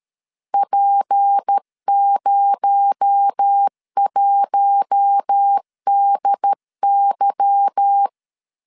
Morse code, heard as a keyed tone, about 13 wpm, 800 Hz. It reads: P01DY
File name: LEZIONI / 8 YJ45 / 8 Y 12wpm 2228